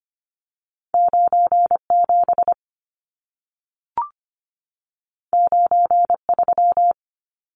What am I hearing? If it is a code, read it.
Morse code, heard as a keyed tone, about 25 wpm, 700 Hz. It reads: 97 93
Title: Numbers - 2-Digit 25wpm